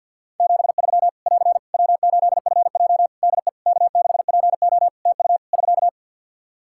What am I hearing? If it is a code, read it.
Morse code, heard as a keyed tone, about 51 wpm, 700 Hz. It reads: 73 <AR> W8FJ DE K6PO TU <SK>
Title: Practice QSO - Exchange 4 - Ending - 50wpm